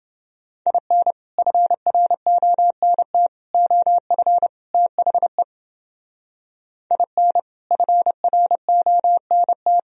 Morse code, heard as a keyed tone, about 30 wpm, 700 Hz. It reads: IN FRONT OF THE IN FRONT